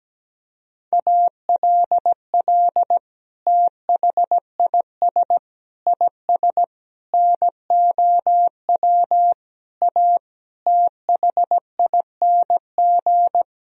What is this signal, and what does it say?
Morse code, heard as a keyed tone, about 17 wpm, 700 Hz. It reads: ALL THIS IS NOW A THING
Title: Sentences from Top 1000 Words - Review Part 3 - 17wpm